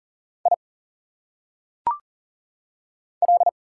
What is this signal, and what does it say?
Morse code, heard as a keyed tone, about 40 wpm, 700 Hz. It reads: I L
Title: Single Letters - Phonetic Pronunciation 40wpm